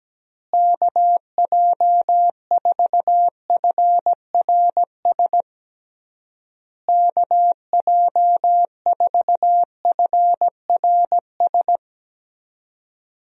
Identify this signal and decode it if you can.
Morse code, heard as a keyed tone, about 17 wpm, 700 Hz. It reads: KJ4FRS KJ4FRS
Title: USA Call Signs - 6 Characters - Repeated 3x - 17wpm